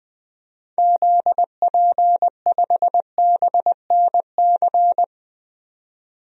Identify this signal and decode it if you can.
Morse code, heard as a keyed tone, about 20 wpm, 700 Hz. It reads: ZP5BNC
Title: DXCC-100 Call Signs - 6 Characters - 20wpm